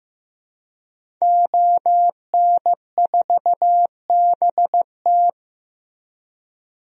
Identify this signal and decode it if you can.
Morse code, heard as a keyed tone, about 15 wpm, 700 Hz. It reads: ON4BT
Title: DXCC-100 Call Signs - 5 Characters - Repeated 2x - 15wpm